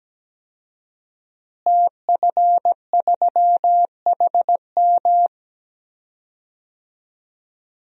Morse code, heard as a keyed tone, about 17 wpm, 700 Hz. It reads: TF3HM
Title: DXCC-100 Call Signs - 5 Characters - Repeated 2x - 17wpm